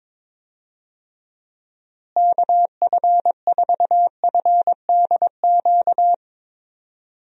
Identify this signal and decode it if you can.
Morse code, heard as a keyed tone, about 22 wpm, 700 Hz. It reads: KF4FDQ
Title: USA Call Signs - 6 Characters - Repeated 3x - 22wpm